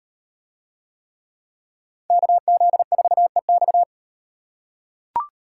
Morse code, heard as a keyed tone, about 37 wpm, 700 Hz. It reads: KZ4EX